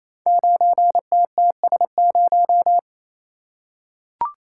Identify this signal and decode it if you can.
Morse code, heard as a keyed tone, about 28 wpm, 700 Hz. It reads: 9TTS0
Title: Code Groups - 5 Letter:Numbers - 28wpm